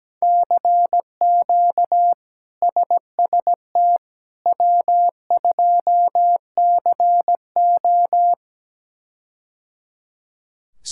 Morse code, heard as a keyed tone, about 17 wpm, 700 Hz. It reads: CQ SST W2CO
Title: K1USN SST QSO Practice - 17wpm